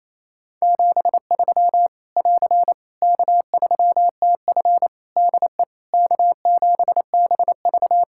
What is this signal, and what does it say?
Morse code, heard as a keyed tone, about 28 wpm, 700 Hz. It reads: 73 <AR> K3TF DE K7BV
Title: Practice QSO - Exchange 4 - Ending - 28wpm